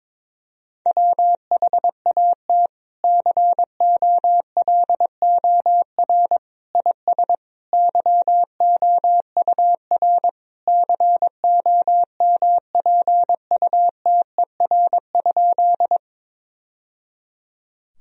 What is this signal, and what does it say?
Morse code, heard as a keyed tone, about 21 wpm, 700 Hz. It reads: WHAT COLOR IS YOUR COMPUTER?
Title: Sentences from Top 1000 Words - Review Part 3 - 22wpm